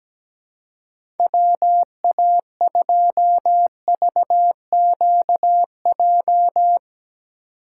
Morse code, heard as a keyed tone, about 17 wpm, 700 Hz. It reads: WA2VQJ
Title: USA Call Signs - 6 Characters - 2x Character Spacing - 17wpm